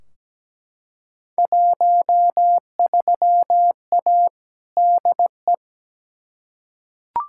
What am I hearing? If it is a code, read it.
Morse code, heard as a keyed tone, about 17 wpm, 700 Hz. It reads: 13A DE
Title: Field Day Exchanges Encore - 17wpm